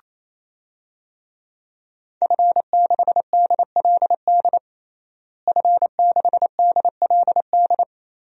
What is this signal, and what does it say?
Morse code, heard as a keyed tone, about 27 wpm, 700 Hz. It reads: F6DLD F6DLD
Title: DXCC-100 Call Signs - 5 Characters - Repeated 3x - 28wpm